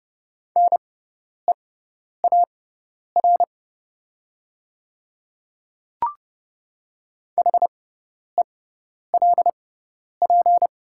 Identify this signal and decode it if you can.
Morse code, heard as a keyed tone, about 30 wpm, spaced as for 10 wpm, 700 Hz. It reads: NEAR HELP
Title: 6x Character Spacing - Top 200 Words - 30wpm